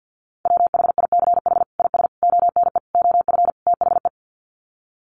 Morse code, heard as a keyed tone, about 50 wpm, 700 Hz. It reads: WHICH IS ONE OF THE
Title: Sets of 5 Words Encore 50wpm